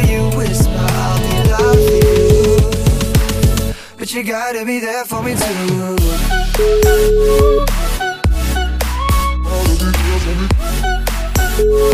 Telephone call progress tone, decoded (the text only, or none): ringback tone